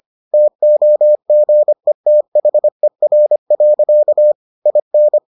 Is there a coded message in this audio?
Yes: Morse code from 0.3 to 5.2 s, heard as a keyed tone, about 25 wpm, 600 Hz. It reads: TOGETHER. IN